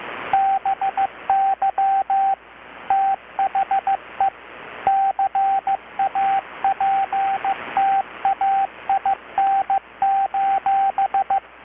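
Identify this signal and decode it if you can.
Morse code, heard as a keyed tone, about 15 wpm, 780 Hz, with radio noise. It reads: BY THE CAPTAIN:
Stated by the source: Morse from KPH's Night of Nights XX maritime radiotelegraphy event, received in Chiapas, Mexico about 3,800 km away on a Kenwood R-600 with a grounded 10 m long-wire antenna.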